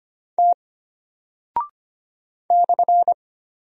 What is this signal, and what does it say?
Morse code, heard as a keyed tone, about 25 wpm, 700 Hz. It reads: T /
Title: Single Character (Letter:Number:Punctuation:Prosign) Alt 25wpm